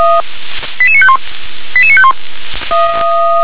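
Five-tone selective call (ZVEI), heard as quick runs of five tones, tones of about 70 ms. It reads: 88951 88951
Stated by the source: Five-tone ZVEI selective call received by SDR.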